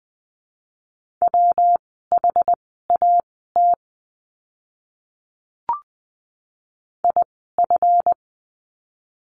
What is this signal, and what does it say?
Morse code, heard as a keyed tone, about 20 wpm, spaced as for 14 wpm, 700 Hz. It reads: WHAT IF
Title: 230 Lesson 37f - Letter x - Words 20wpm